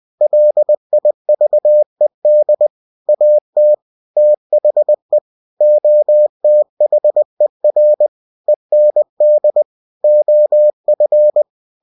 Morse code, heard as a keyed tone, about 20 wpm, 600 Hz. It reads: LIVED AT THE OTHER END OF
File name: cw-books / winnie-the-pooh / winnie-20wpm-page-0017